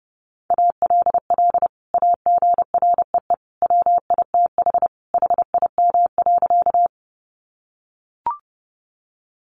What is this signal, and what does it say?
Morse code, heard as a keyed tone, about 30 wpm, 700 Hz. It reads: ALL AGREE WITH HIM.
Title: Sentences from Top 1100 Words 30wpm